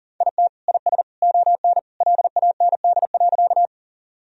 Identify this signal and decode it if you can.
Morse code, heard as a keyed tone, about 40 wpm, 700 Hz. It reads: IT IS ON LAND.
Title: Sentences from Top 1000 Words - Review Part 1 - 40wpm